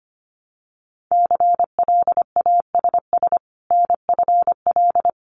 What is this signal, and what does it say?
Morse code, heard as a keyed tone, about 25 wpm, 700 Hz. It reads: CLASS NFL
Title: Winter Field Day Exchanges 25wpm